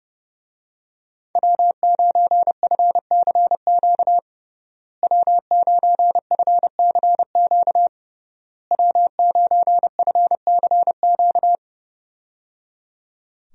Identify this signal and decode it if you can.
Morse code, heard as a keyed tone, about 30 wpm, 700 Hz. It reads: W9FCQ W9FCQ W9FCQ
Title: USA Call Signs - 5 Characters - Repeated 3x - 30wpm